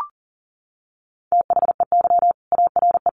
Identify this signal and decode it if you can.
Morse code, heard as a keyed tone, about 40 wpm, 700 Hz. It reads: THEY ARE
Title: Sentences from Top 900 Words 40wpm